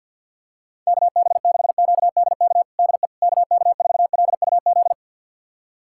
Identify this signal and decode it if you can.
Morse code, heard as a keyed tone, about 50 wpm, 700 Hz. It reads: KB6QDK DE KK4LUZ
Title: USA Callsign DE Callsign 50wpm